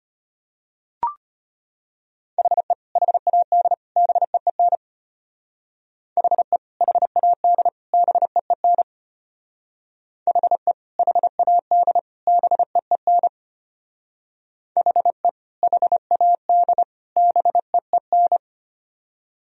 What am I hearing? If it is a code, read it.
Morse code, heard as a keyed tone, about 31 wpm, 700 Hz. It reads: HE HAD BEEN HE HAD BEEN HE HAD BEEN HE HAD BEEN
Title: Speed-Racing - Sets of 3 Words 25wpm